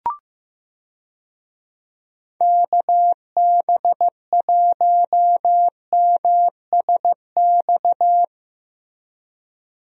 Morse code, heard as a keyed tone, about 15 wpm, 700 Hz. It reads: KB1MSX
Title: USA Call Signs - 6 Characters - Repeated 3x - 15wpm